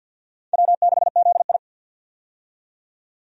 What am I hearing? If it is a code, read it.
Morse code, heard as a keyed tone, about 52 wpm, 700 Hz. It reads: W6ZI
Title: USA Call Signs - 4 Characters - 50wpm